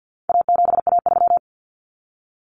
Morse code, heard as a keyed tone, about 48 wpm, 700 Hz. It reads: A7A3
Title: Common Cut Numbers - 4-Digit 50wpm